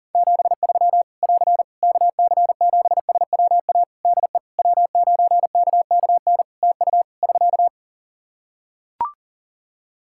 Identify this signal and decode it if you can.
Morse code, heard as a keyed tone, about 40 wpm, 700 Hz. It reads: 73 <AR> KC7SWA DE W9KKN TU <SK>
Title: Practice QSO - Exchange 4 - Ending - 40wpm